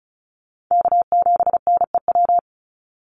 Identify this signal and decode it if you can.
Morse code, heard as a keyed tone, about 34 wpm, 700 Hz. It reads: K7NEW
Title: NA SOTA Chaser Callsigns 35wpm